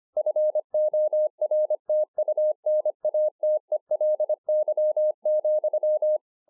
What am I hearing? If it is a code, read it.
Morse code, heard as a keyed tone, about 24 wpm, 610 Hz. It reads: FORTUNATELY,
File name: 2011-04-mp3 / THE 3.11 3 25WPM